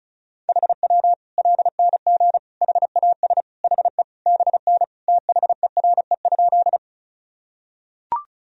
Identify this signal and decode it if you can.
Morse code, heard as a keyed tone, about 36 wpm, 700 Hz. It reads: HW LNG HAS HE BN THERE?
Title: Sentences from Top 100 Words and Abbreviations 35wpm